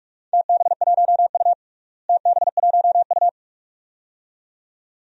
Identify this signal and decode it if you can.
Morse code, heard as a keyed tone, about 44 wpm, 700 Hz. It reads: TB1U TB1U
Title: DXCC-100 Call Signs - 4 Characters - Repeated 2x - 45wpm